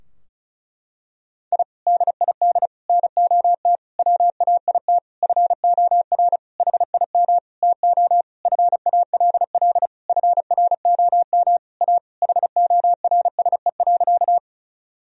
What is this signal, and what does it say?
Morse code, heard as a keyed tone, about 34 wpm, 700 Hz. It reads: I DID NOT WAIT FOR HIM TO FALL FROM A HORSE.
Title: Sentences from Top 1000 Words - Review Part 2 - 35wpm